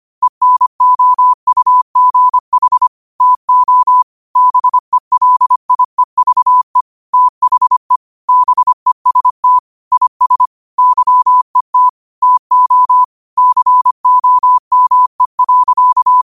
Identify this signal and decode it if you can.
Morse code, heard as a keyed tone, about 25 wpm, 1000 Hz. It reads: ENOUGH TO BELIEVE THE BEST IS YET TO COME.
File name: qotdfiles / 25wpm